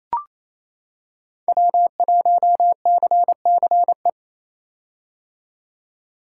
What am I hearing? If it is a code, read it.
Morse code, heard as a keyed tone, about 28 wpm, 700 Hz. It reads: W1CCE
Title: K1USN SST QSO Practice - 28wpm